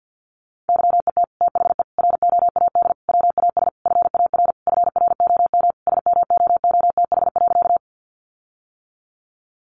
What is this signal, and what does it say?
Morse code, heard as a keyed tone, about 50 wpm, 700 Hz. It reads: YET THE ROAD WAS FAR FROM SMOOTH.
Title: Sentences from Top 1400 Words 50wpm